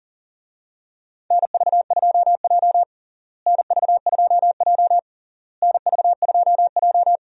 Morse code, heard as a keyed tone, about 37 wpm, 700 Hz. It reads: NV2J NV2J NV2J
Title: USA Call Signs - 4 Characters - Repeated 3x - 40wpm